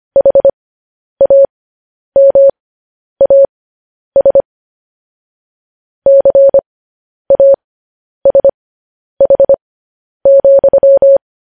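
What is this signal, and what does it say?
Morse code, heard as a keyed tone, about 25 wpm, spaced as for 10 wpm, 550 Hz. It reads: HAMAS CASH,
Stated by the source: News headlines sent in Morse at 10 wpm.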